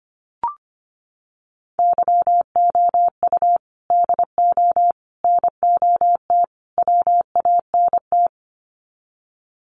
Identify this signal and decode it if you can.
Morse code, heard as a keyed tone, about 25 wpm, 700 Hz. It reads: YOU DO NOT WANT
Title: Sets of 4 Words 25wpm